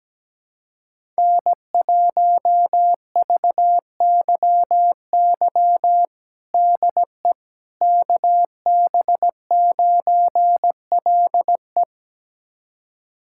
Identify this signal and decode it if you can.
Morse code, heard as a keyed tone, about 17 wpm, 700 Hz. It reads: N1VYY DE KB9LE
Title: USA Callsign DE Callsign - 5 Characters - 17wpm